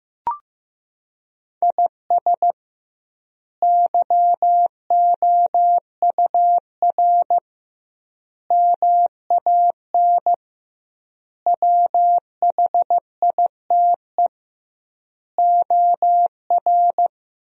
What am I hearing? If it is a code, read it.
Morse code, heard as a keyed tone, about 15 wpm, 700 Hz. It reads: IS YOUR MAN WHITE OR
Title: 2x Word-Spacing - Sentences from Top 400 Words - 15wpm